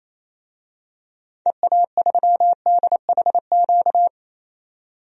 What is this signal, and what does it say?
Morse code, heard as a keyed tone, about 28 wpm, 700 Hz. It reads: EA3DHQ
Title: DXCC-100 Call Signs - 6 Characters - 28wpm